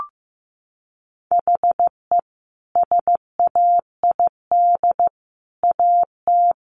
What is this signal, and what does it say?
Morse code, heard as a keyed tone, about 15 wpm, 700 Hz. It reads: HE SAID AT
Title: Sentences from Top 300 Words 15wpm